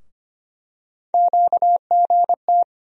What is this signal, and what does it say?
Morse code, heard as a keyed tone, about 25 wpm, 700 Hz. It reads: QGT
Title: USA Call Sign Postfixes - 3 Letters 25wpm